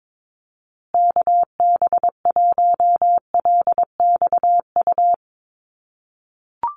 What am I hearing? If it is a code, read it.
Morse code, heard as a keyed tone, about 22 wpm, 700 Hz. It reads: KB1LXU
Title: USA Call Signs Repeated 3x 22wpm